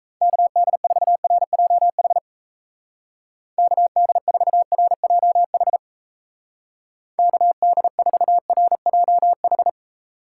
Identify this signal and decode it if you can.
Morse code, heard as a keyed tone, about 35 wpm, 700 Hz. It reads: KD4RJH KD4RJH KD4RJH
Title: Speed Racing - USA 2x3 Call Signs 28wpm